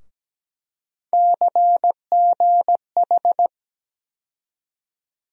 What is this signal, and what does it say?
Morse code, heard as a keyed tone, about 17 wpm, 700 Hz. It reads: CGH
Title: USA Call Sign Postfixes - 3 Letters 17wpm